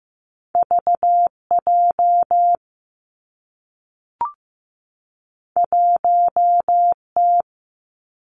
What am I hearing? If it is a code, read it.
Morse code, heard as a keyed tone, about 15 wpm, 700 Hz. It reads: V J 1 T
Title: Two Challenging Letter-Number Combinations 15wpm